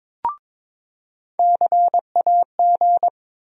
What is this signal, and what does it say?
Morse code, heard as a keyed tone, about 22 wpm, 700 Hz. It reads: CAG
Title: USA Call Sign Postfixes - 3 Letters 22wpm